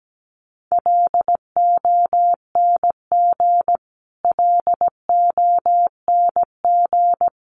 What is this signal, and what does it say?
Morse code, heard as a keyed tone, about 17 wpm, 700 Hz. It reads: LONG LONG
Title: Top 500 Words Repeated 17wpm